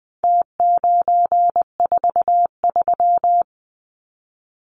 Morse code, heard as a keyed tone, about 20 wpm, 700 Hz. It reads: T943
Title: Numbers - Contest Style - 4-Digit 20wpm